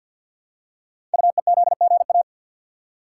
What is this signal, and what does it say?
Morse code, heard as a keyed tone, about 46 wpm, 690 Hz. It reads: UEZGA
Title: Code Groups - 5 Letter:Numbers - 50wpm